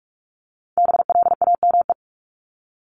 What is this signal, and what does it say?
Morse code, heard as a keyed tone, about 44 wpm, 700 Hz. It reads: BLAME
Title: Top 1751-2000 Words 45wpm